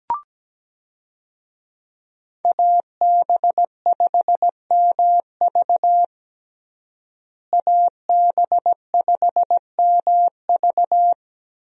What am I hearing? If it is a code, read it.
Morse code, heard as a keyed tone, about 17 wpm, 700 Hz. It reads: AB5MV AB5MV
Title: USA Call Signs - 5 Characters - Repeated 3x - 17wpm